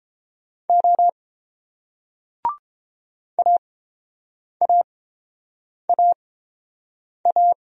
Morse code, heard as a keyed tone, about 31 wpm, spaced as for 7 wpm, 700 Hz. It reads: O AAAA